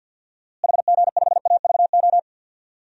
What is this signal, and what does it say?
Morse code, heard as a keyed tone, about 52 wpm, 700 Hz. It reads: HG5AVO
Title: DXCC-100 Call Signs - 6 Characters - Repeated 2x - 50wpm